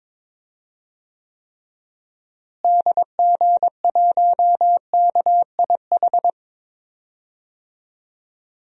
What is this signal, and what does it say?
Morse code, heard as a keyed tone, about 21 wpm, 700 Hz. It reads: DG1KIH
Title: DXCC-100 Call Signs - 6 Characters - Repeated 3x - 22wpm